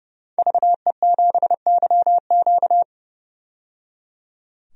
Morse code, heard as a keyed tone, about 30 wpm, 700 Hz. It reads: VE7YQ
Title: DXCC-100 Call Signs - 5 Characters - 30wpm